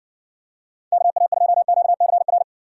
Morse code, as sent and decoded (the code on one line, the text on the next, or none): -.- .- ..--- --.- -.-. .-.
KA2QCR